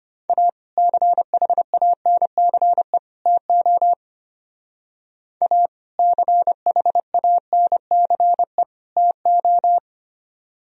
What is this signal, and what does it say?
Morse code, heard as a keyed tone, about 31 wpm, 700 Hz. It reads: A CHANCE TO A CHANCE TO
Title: Speed-Racing - Sets of 3 Words 25wpm